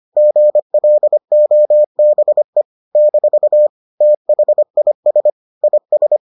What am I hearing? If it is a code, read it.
Morse code, heard as a keyed tone, about 25 wpm, 600 Hz. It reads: GLOBE - THIS IS